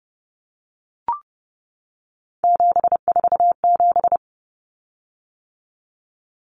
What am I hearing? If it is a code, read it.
Morse code, heard as a keyed tone, about 30 wpm, 700 Hz. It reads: 747
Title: Common Cut Numbers - 3-Digit 30wpm